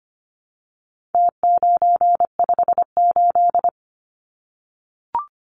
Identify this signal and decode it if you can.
Morse code, heard as a keyed tone, about 25 wpm, 700 Hz. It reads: T958
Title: Numbers - Contest Style - 4-Digit 25wpm